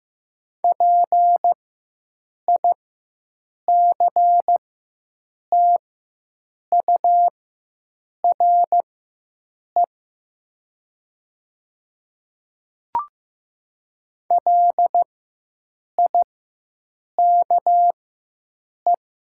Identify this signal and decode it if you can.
Morse code, heard as a keyed tone, about 15 wpm, spaced as for 7 wpm, 700 Hz. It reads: PICTURE LIKE
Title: 4x Character Spacing - Top 200 Words - 15wpm